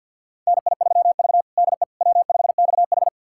Morse code, heard as a keyed tone, about 51 wpm, 700 Hz. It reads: NI3V DE W5XH
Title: USA Callsign DE Callsign - 4 Characters - 50wpm